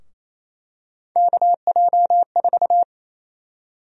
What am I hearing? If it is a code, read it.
Morse code, heard as a keyed tone, about 28 wpm, 700 Hz. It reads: KJ4
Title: USA Call Sign Prefixes - Repeated 2x - 28wpm